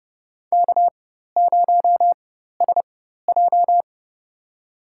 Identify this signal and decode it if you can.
Morse code, heard as a keyed tone, about 30 wpm, spaced as for 14 wpm, 700 Hz. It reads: K0SJ
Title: USA Call Signs - 4 Characters - 4x Character Spacing - 30wpm